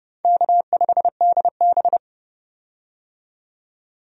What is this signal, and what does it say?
Morse code, heard as a keyed tone, about 32 wpm, 700 Hz. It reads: K5DB
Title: USA Call Signs - 4 Characters - 30wpm